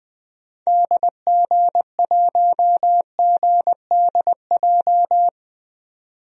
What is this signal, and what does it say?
Morse code, heard as a keyed tone, about 20 wpm, 700 Hz. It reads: DG1GDJ
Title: DXCC-100 Call Signs - 6 Characters - 20wpm